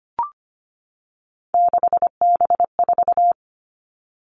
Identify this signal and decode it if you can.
Morse code, heard as a keyed tone, about 25 wpm, 700 Hz. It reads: 6B4
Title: All Cut Numbers - 3-Digit 25wpm